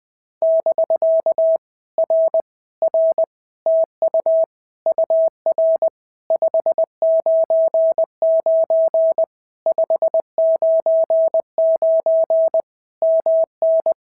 Morse code, heard as a keyed tone, about 20 wpm, 650 Hz. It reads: <BK> R R TU UR 599 599 MN
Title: POTA QSO Practice - 20wpm